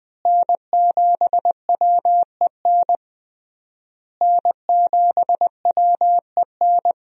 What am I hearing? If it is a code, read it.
Morse code, heard as a keyed tone, about 20 wpm, 700 Hz. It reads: N7WEN N7WEN
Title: USA Call Signs - 5 Characters - Repeated 3x - 20wpm